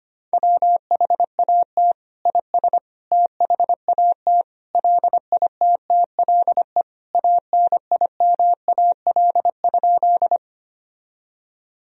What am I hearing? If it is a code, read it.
Morse code, heard as a keyed tone, about 25 wpm, 700 Hz. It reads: WHAT IS THAT LITTLE ANIMAL?
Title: Sentences from Top 1000 Words - Review Part 1 - 25wpm